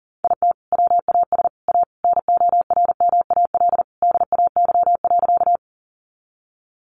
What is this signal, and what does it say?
Morse code, heard as a keyed tone, about 40 wpm, 700 Hz. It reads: IT WAS A NORMAL DAY.
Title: Sentences from Top 1200 Words 40wpm